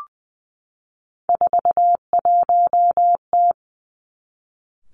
Morse code, heard as a keyed tone, about 20 wpm, 700 Hz. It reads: 41T
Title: Three Challenging Letter-Number Combinations 20wpm